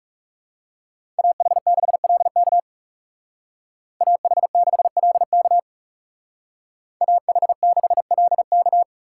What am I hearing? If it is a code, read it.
Morse code, heard as a keyed tone, about 35 wpm, 700 Hz. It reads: AH6LK AH6LK AH6LK
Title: Speed Racing - USA 2x2 Call Signs 30wpm